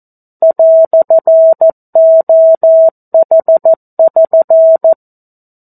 Morse code, heard as a keyed tone, about 14 wpm, 650 Hz. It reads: "OH<SN>
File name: 02-Chapter-02 14 WPM